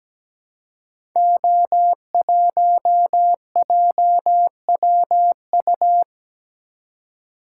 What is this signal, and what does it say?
Morse code, heard as a keyed tone, about 17 wpm, 700 Hz. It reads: O1JWU
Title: Code Groups - 5 Letter:Numbers - 17wpm